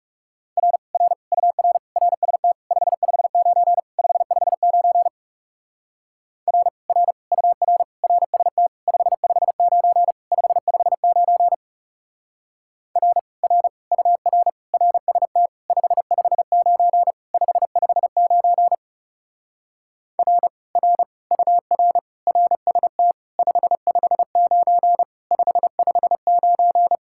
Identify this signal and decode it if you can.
Morse code, heard as a keyed tone, about 37 wpm, 700 Hz. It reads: R R UR RST 559 559 R R UR RST 559 559 R R UR RST 559 559 R R UR RST 559 559
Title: Speed-Racing Common QSO Phrases - 30wpm